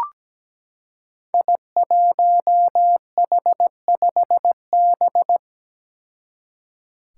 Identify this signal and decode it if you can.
Morse code, heard as a keyed tone, about 17 wpm, 700 Hz. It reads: I1H5B
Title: Five Challenging Letter-Number Combinations 17wpm